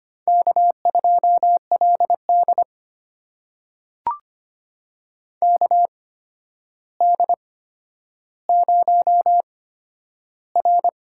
Morse code, heard as a keyed tone, about 25 wpm, spaced as for 11 wpm, 700 Hz. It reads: K2LD KD0R
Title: USA Call Signs - 4 Characters - 8x Character Spacing - 25wpm